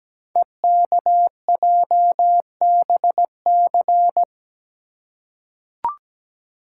Morse code, heard as a keyed tone, about 17 wpm, 700 Hz. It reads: EKJBC
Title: Code Groups - 5 Letters - 17wpm